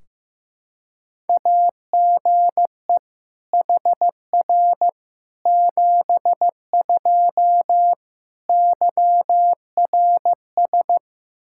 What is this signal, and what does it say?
Morse code, heard as a keyed tone, about 15 wpm, 700 Hz. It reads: AGE HR 72 YRS
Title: Practice QSO - Exchange 3 - More Info - 15wpm